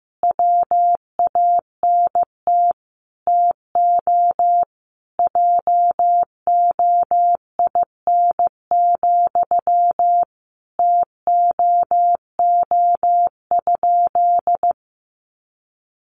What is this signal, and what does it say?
Morse code, heard as a keyed tone, about 15 wpm, 700 Hz. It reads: WANT TO JOIN, TOO?
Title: Sentences from Top 1400 Words 15wpm